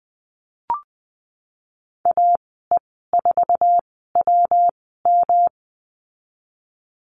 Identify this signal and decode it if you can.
Morse code, heard as a keyed tone, about 20 wpm, spaced as for 15 wpm, 700 Hz. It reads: AE4WM
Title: 122 Lesson 20g - Number 3 - Call Signs 20wpm